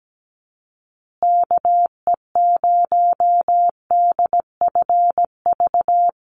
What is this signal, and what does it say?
Morse code, heard as a keyed tone, about 17 wpm, 700 Hz. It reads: KE0DFV
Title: USA Call Signs Repeated 3x 17wpm